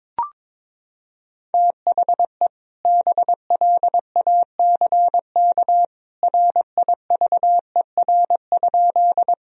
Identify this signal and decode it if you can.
Morse code, heard as a keyed tone, about 22 wpm, 700 Hz. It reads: THE BLACK RIVER?
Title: Sentences from Top 1000 Words - Review Part 2 - 22wpm